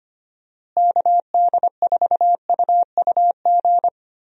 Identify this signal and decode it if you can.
Morse code, heard as a keyed tone, about 25 wpm, 700 Hz. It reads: KD4UUG
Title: USA Call Signs - 6 Characters - Repeated 2x - 25wpm